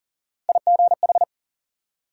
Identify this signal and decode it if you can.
Morse code, heard as a keyed tone, about 53 wpm, 700 Hz. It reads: IGH